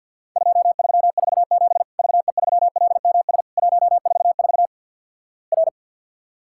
Morse code, heard as a keyed tone, about 50 wpm, 700 Hz. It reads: 2347 VE3LMS 144 R
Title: Worked All Europe DX Contest - QTC Exchange - 50wpm